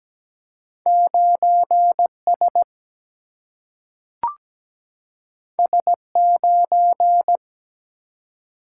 Morse code, heard as a keyed tone, about 17 wpm, 700 Hz. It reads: 9 S S 9
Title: Single-Letter Single-Number 17wpm